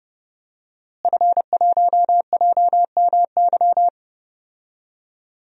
Morse code, heard as a keyed tone, about 32 wpm, 700 Hz. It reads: F1JMY
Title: DXCC-100 Call Signs - 5 Characters - Repeated 2x - 30wpm